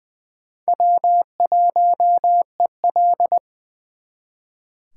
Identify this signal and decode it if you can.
Morse code, heard as a keyed tone, about 23 wpm, 700 Hz. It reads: W1EL
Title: Speed Racing - USA 1x2 Call Signs 20wpm